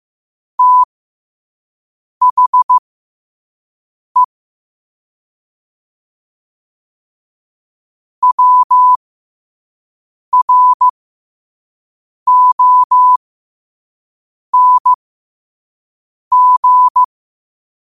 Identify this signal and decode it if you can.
Morse code, heard as a keyed tone, about 15 wpm, spaced as for 5 wpm, 1000 Hz. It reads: THE WRONG
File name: qotdfiles / 5wpm